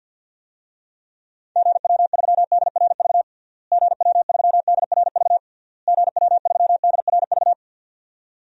Morse code, heard as a keyed tone, about 50 wpm, 700 Hz. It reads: GW3DRV GW3DRV GW3DRV
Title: DXCC-100 Call Signs - 6 Characters - Repeated 3x - 50wpm